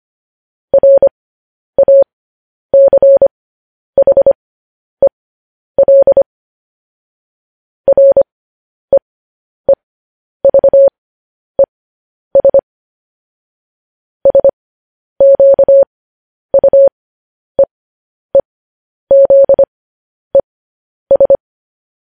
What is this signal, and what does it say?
Morse code, heard as a keyed tone, about 25 wpm, spaced as for 10 wpm, 550 Hz. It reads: RACHEL REEVES SQUEEZES
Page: news-headlines-in-morse-code-at-10-wpm